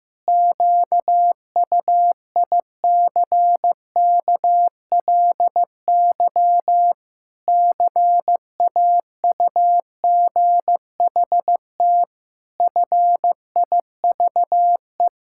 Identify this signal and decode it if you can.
Morse code, heard as a keyed tone, about 14 wpm, 700 Hz. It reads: QUICKLY CAUGHT FIVE
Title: Pangram Challenge v2 - 15wpm